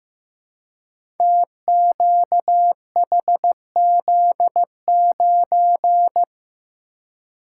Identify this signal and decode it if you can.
Morse code, heard as a keyed tone, about 15 wpm, 700 Hz. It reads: TQHZ9
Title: Code Groups - 5 Letter:Numbers - 15wpm